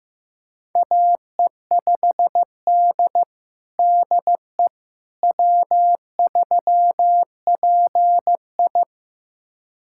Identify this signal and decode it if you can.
Morse code, heard as a keyed tone, about 15 wpm, 700 Hz. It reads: AE5D DE W3PI
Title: USA Callsign DE Callsign - 4 Characters - 15wpm